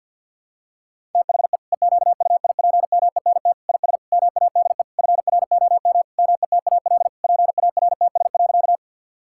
Morse code, heard as a keyed tone, about 47 wpm, 700 Hz. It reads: THE EQUIPMENT IS MADE FROM METAL PARTS.
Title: Sentences from Top 1000 Words - Review Part 6 - 50wpm